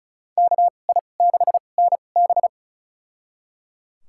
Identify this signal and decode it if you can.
Morse code, heard as a keyed tone, about 36 wpm, spaced as for 25 wpm, 700 Hz. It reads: KI6NB
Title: USA Call Signs - 5 Characters - 2x Character Spacing - 35wpm